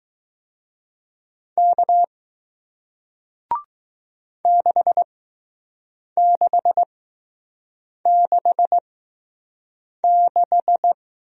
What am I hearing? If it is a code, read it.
Morse code, heard as a keyed tone, about 20 wpm, spaced as for 6 wpm, 700 Hz. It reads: K 6666